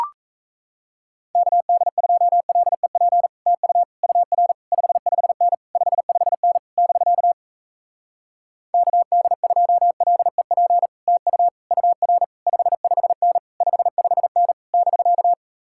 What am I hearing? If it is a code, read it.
Morse code, heard as a keyed tone, about 38 wpm, 700 Hz. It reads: KD2LEP TU UR 55N 55N <BK> KD2LEP TU UR 55N 55N <BK>